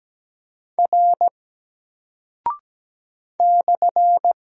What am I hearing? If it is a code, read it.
Morse code, heard as a keyed tone, about 17 wpm, 700 Hz. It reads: R /
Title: Single Character (Letter:Number:Punctuation:Prosign) Alt 17wpm